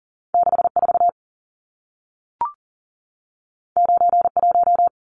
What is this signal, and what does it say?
Morse code, heard as a keyed tone, about 40 wpm, 700 Hz. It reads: 64 91
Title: Numbers - 2-Digit 40wpm